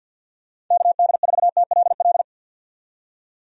Morse code, heard as a keyed tone, about 50 wpm, 700 Hz. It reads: KD4TLL
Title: USA Call Signs - 6 Characters - 50wpm